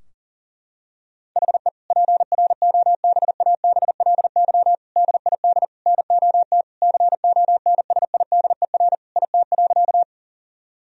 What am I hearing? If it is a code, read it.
Morse code, heard as a keyed tone, about 40 wpm, 690 Hz. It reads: HE PROBABLY DID NOT CONSIDER IT.
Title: Sentences from Top 1000 Words - Review Part 4 - 40wpm